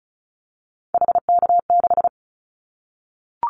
Morse code, heard as a keyed tone, about 34 wpm, 700 Hz. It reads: HK6
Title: Callsign Prefixes for Top 100 DXCC - 35wpm